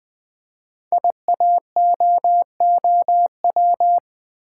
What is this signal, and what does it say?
Morse code, heard as a keyed tone, about 20 wpm, 700 Hz. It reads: IAOOW
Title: Code Groups - 5 Letters - 20wpm